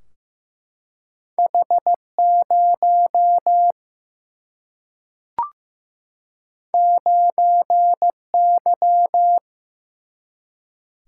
Morse code, heard as a keyed tone, about 16 wpm, 700 Hz. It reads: H0 9Y
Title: Single-Letter Single-Number 15wpm